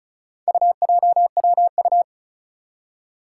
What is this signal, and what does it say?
Morse code, heard as a keyed tone, about 36 wpm, 700 Hz. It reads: UJWU
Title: Four Challenging Letter-Number Combinations 35wpm